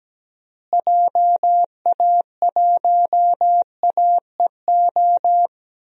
Morse code, heard as a keyed tone, about 17 wpm, 700 Hz. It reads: JA1AEO
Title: DXCC-100 Call Signs - 6 Characters - Repeated 2x - 17wpm